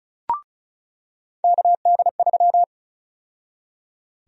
Morse code, heard as a keyed tone, about 36 wpm, 700 Hz. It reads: KD3
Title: USA Call Sign Prefixes - 35wpm